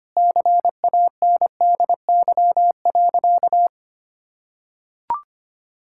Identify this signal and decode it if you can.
Morse code, heard as a keyed tone, about 25 wpm, 700 Hz. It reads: CANDY.
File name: practice / 25wpm / Sentences with 4-Word Phrases - 25wpm